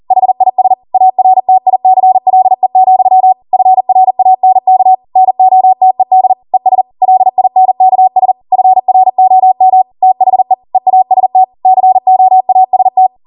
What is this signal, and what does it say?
Morse code, heard as a keyed tone, about 40 wpm, 750 Hz. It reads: HIS ARTICLE, FRANK NOTED ES LINKS FROM THE EAST COAST